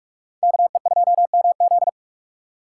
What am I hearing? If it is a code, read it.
Morse code, heard as a keyed tone, about 41 wpm, 690 Hz. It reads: KE2MZ
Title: USA Call Signs - 4x Character Spacing - 45wpm